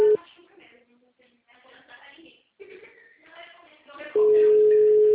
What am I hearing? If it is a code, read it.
Ringback tone (Europe, 425 Hz), with speech.
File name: out- -2024- - - .1042